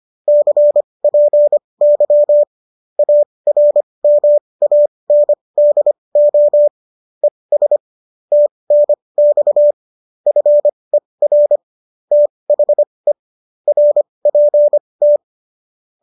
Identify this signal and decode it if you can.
Morse code, heard as a keyed tone, about 25 wpm, spaced as for 20 wpm, 590 Hz. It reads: CPY ARMANDO ES TNX FER THE RPT